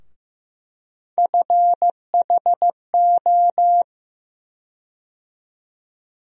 Morse code, heard as a keyed tone, about 15 wpm, 700 Hz. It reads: FHO
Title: USA Call Sign Postfixes - 3 Letters 15wpm